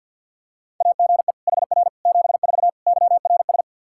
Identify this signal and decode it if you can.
Morse code, heard as a keyed tone, about 52 wpm, 700 Hz. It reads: AGE HR 74 YRS
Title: Practice QSO - Exchange 3 - More Info - 50wpm